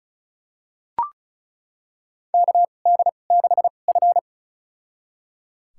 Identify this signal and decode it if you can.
Morse code, heard as a keyed tone, about 36 wpm, spaced as for 25 wpm, 700 Hz. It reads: KD6F
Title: USA Call Signs - 4 Characters - 2x Character Spacing - 35wpm